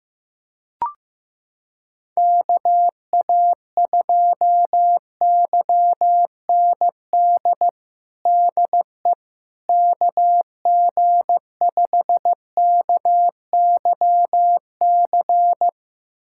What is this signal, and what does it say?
Morse code, heard as a keyed tone, about 16 wpm, 700 Hz. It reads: KA2YND DE KG5KYC
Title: USA Callsign DE Callsign - 6 Characters - 15wpm